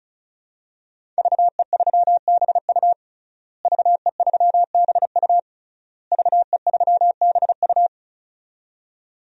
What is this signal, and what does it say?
Morse code, heard as a keyed tone, about 36 wpm, 700 Hz. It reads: VE3BU VE3BU VE3BU
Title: DXCC-100 Call Signs - 5 Characters - Repeated 3x - 35wpm